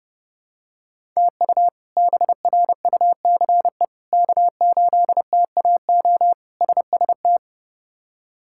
Morse code, heard as a keyed tone, about 32 wpm, 700 Hz. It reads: TU BRUCE K8TAO SST
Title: K1USN SST QSO Practice - 30wpm